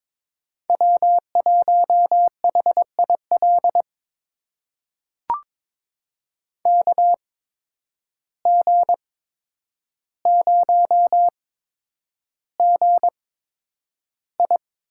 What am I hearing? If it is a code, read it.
Morse code, heard as a keyed tone, about 22 wpm, spaced as for 9 wpm, 700 Hz. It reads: W1HIL KG0GI
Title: USA Call Signs - 5 Characters - 8x Character Spacing - 22wpm